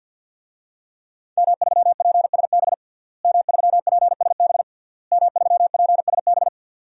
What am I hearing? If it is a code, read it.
Morse code, heard as a keyed tone, about 51 wpm, 700 Hz. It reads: M3PSB M3PSB M3PSB
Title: DXCC-100 Call Signs - 5 Characters - Repeated 3x - 50wpm